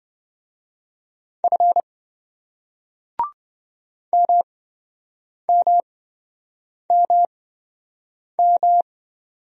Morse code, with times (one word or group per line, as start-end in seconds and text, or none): 1.4-1.8 F
4.1-8.8 MMMM